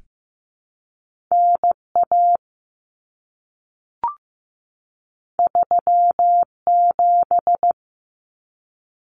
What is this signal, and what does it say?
Morse code, heard as a keyed tone, about 15 wpm, 700 Hz. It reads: NA 37
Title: Common Cut Numbers - 2-Digit 15wpm